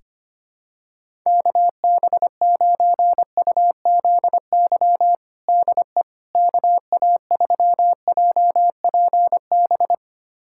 Morse code, heard as a keyed tone, about 25 wpm, 700 Hz. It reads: KB9UZY DE KA3JPB
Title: USA Callsign DE Callsign - 6 Characters - 25wpm